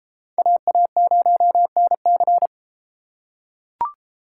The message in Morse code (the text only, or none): AA0NC